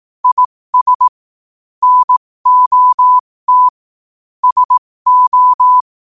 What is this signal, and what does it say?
Morse code, heard as a keyed tone, about 18 wpm, spaced as for 15 wpm, 1000 Hz. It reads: IS NOT SO
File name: qotdfiles / 15wpm